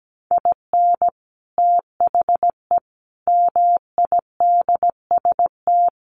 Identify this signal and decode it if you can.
Morse code, heard as a keyed tone, about 17 wpm, 700 Hz. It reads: IN THE MIDST
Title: Sets of 5 Words 17wpm